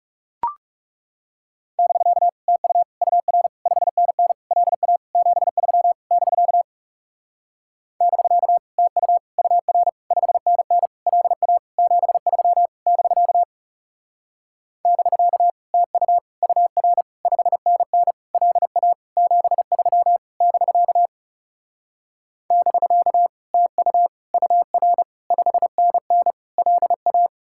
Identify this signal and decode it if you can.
Morse code, heard as a keyed tone, about 39 wpm, 700 Hz. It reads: <BK> TU UR 5NN LA 73 <BK> <BK> TU UR 5NN LA 73 <BK> <BK> TU UR 5NN LA 73 <BK> <BK> TU UR 5NN LA